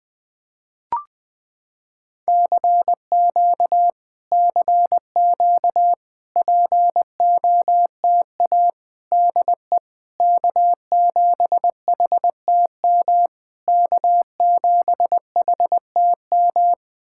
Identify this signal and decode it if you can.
Morse code, heard as a keyed tone, about 20 wpm, 700 Hz. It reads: CQ CQ POTA DE K7HTM K7HTM
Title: POTA QSO Practice - 20wpm